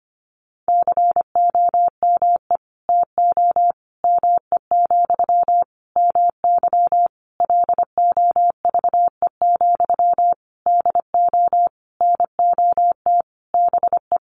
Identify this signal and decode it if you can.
Morse code, heard as a keyed tone, about 25 wpm, 700 Hz. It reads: COME TO ME, MY LOVE, DO NOT BE
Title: Sentences from Top 1500 Words 25wpm